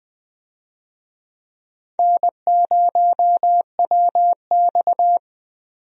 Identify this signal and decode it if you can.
Morse code, heard as a keyed tone, about 20 wpm, 700 Hz. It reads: N0WX
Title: USA Call Signs - 4 Characters - Repeated 3x - 20wpm